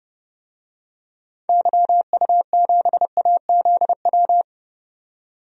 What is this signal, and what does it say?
Morse code, heard as a keyed tone, about 32 wpm, 700 Hz. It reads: YU7AZW
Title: DXCC-100 Call Signs - 6 Characters - 30wpm